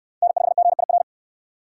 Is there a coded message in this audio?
Yes: Morse code from 0.2 to 1.0 s, heard as a keyed tone, about 62 wpm, 700 Hz. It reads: N5GIR